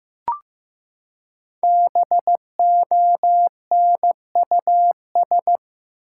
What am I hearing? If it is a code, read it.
Morse code, heard as a keyed tone, about 16 wpm, 700 Hz. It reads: BONUS
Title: Five Letter Words - 15wpm